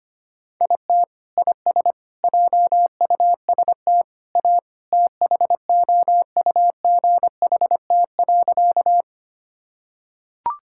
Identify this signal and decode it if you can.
Morse code, heard as a keyed tone, about 25 wpm, 700 Hz. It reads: IT IS JUST A THOUGHT.
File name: practice / 25wpm / Sentences with 2-Word Phrases - 25wpm